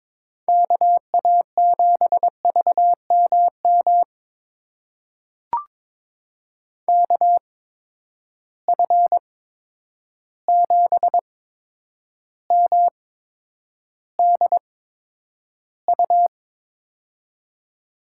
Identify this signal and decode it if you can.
Morse code, heard as a keyed tone, about 22 wpm, spaced as for 9 wpm, 700 Hz. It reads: KA7VMM KF7MDU
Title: USA Call Signs - 6 Characters - 8x Character Spacing - 22wpm